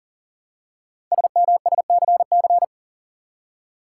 Morse code, heard as a keyed tone, about 44 wpm, 700 Hz. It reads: SMSCC
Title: Code Groups - 5 Letters - 40wpm